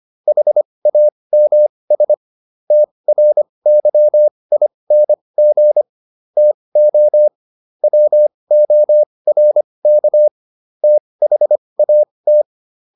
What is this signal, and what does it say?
Morse code, heard as a keyed tone, about 25 wpm, spaced as for 20 wpm, 600 Hz. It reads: HAMS TRYING TO WORK THAT